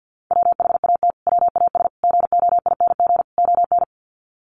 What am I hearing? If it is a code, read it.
Morse code, heard as a keyed tone, about 48 wpm, 700 Hz. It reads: WHAT WAS GOING ON
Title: Sets of 4 Words 50wpm